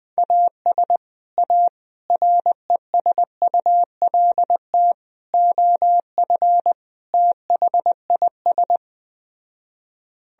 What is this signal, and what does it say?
Morse code, heard as a keyed tone, about 20 wpm, 700 Hz. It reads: AS A RESULT OF THIS